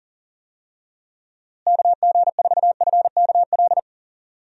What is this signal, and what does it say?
Morse code, heard as a keyed tone, about 40 wpm, 700 Hz. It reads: KG4FKL
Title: USA Call Signs - 6 Characters - Repeated 2x - 40wpm